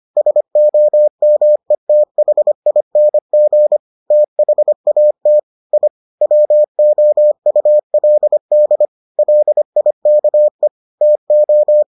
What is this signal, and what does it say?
Morse code, heard as a keyed tone, about 25 wpm, 600 Hz. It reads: SOMETHING THAT I WOULD LIKE TO